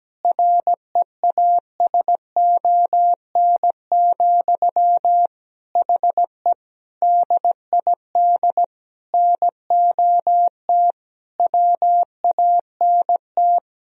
Morse code, heard as a keyed tone, about 17 wpm, 700 Hz. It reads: REASON, HE DID NOT WANT
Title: Sentences from Top 1000 Words - Review Part 3 - 17wpm